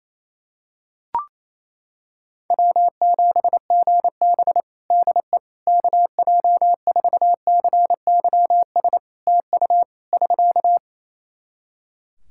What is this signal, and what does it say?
Morse code, heard as a keyed tone, about 28 wpm, 700 Hz. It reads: W7GB DE KJ4CYS TU <SK>
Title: Practice QSO - Exchange 4 - Ending - 28wpm